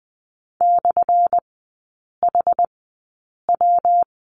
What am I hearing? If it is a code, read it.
Morse code, heard as a keyed tone, about 20 wpm, spaced as for 8 wpm, 700 Hz. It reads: /HW
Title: 178 Lesson 29e - Slash - Three Characters 20wpm